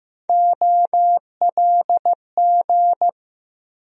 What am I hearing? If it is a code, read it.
Morse code, heard as a keyed tone, about 15 wpm, 700 Hz. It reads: OLG
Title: USA Call Sign Postfixes - 3 Letters 15wpm